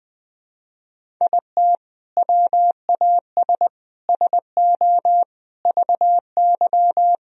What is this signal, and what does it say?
Morse code, heard as a keyed tone, about 21 wpm, 700 Hz. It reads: IT WAS SO VY
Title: Sentences from Top 500 Words and Abbreviations 20wpm